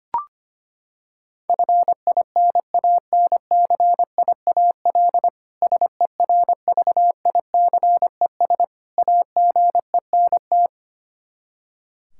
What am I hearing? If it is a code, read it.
Morse code, heard as a keyed tone, about 25 wpm, 700 Hz. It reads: FINANCIAL SERVICES AGENT
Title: K1USN SST QSO Practice 25wpm